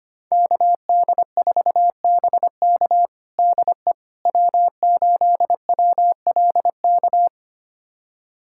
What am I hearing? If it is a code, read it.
Morse code, heard as a keyed tone, about 25 wpm, 700 Hz. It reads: KD4BK DE W8WLK
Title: USA Callsign DE Callsign 25wpm